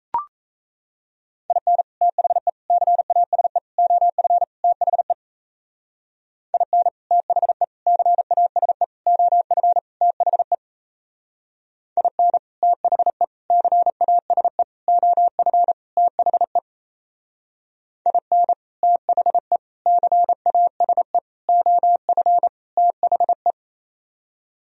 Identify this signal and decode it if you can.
Morse code, heard as a keyed tone, about 34 wpm, 700 Hz. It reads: IN THE CASE OF THE IN THE CASE OF THE IN THE CASE OF THE IN THE CASE OF THE
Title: Speed-Racing - Sets of 5 Words 28wpm